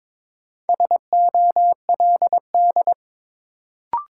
Morse code, heard as a keyed tone, about 23 wpm, 700 Hz. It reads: SOLD
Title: Sentences from Top 1400 Words - New Word Drill Alt - 22wpm